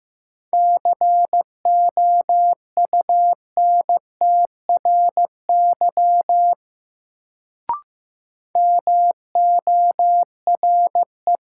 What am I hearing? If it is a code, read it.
Morse code, heard as a keyed tone, about 15 wpm, 700 Hz. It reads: COUNTRY MORE
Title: Sets of 2 Words - New Word Drill - 15wpm